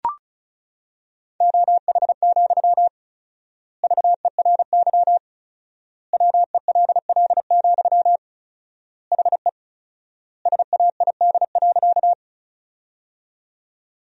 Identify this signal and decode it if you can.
Morse code, heard as a keyed tone, about 34 wpm, 700 Hz. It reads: OH, VERY WELL, HE SAID.
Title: 4x Word-Spacing - Sentences from Top 500 Words - 35wpm